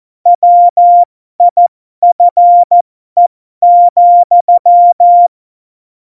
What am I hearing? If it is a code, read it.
Morse code, heard as a keyed tone, about 14 wpm, spaced as for 12 wpm, 700 Hz. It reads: WIFE,
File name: Morsekurs / Witze